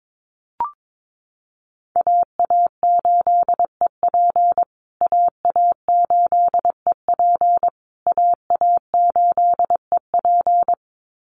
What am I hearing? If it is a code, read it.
Morse code, heard as a keyed tone, about 22 wpm, 700 Hz. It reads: AA8EP AA8EP AA8EP
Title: USA Call Signs Repeated 3x 22wpm